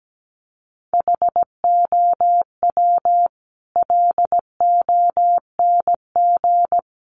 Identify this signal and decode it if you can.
Morse code, heard as a keyed tone, about 17 wpm, 700 Hz. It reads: HOW LONG
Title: Sentences from Top 500 Words 17wpm